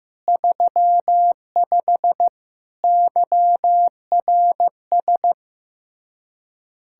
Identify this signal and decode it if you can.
Morse code, heard as a keyed tone, about 15 wpm, 700 Hz. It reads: 35 YRS
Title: Practice QSO - Exchange 3 - More Info - 15wpm